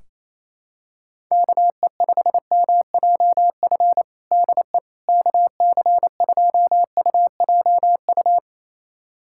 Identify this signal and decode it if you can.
Morse code, heard as a keyed tone, about 28 wpm, 700 Hz. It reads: KE5MJF DE KC2UJU
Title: USA Callsign DE Callsign - 6 Characters - 28wpm